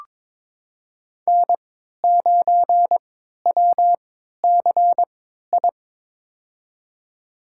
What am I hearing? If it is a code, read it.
Morse code, heard as a keyed tone, about 22 wpm, spaced as for 12 wpm, 700 Hz. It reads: N9WCI
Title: USA Call Signs - 3x Character Spacing - 22wpm